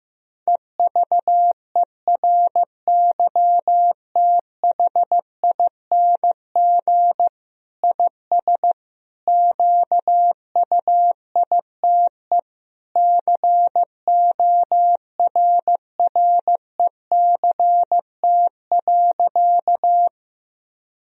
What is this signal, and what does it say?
Morse code, heard as a keyed tone, about 15 wpm, 700 Hz. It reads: EVERYTHING IS QUITE CORRECT.
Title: Sentences from Top 1000 Words - Review Part 2 - 15wpm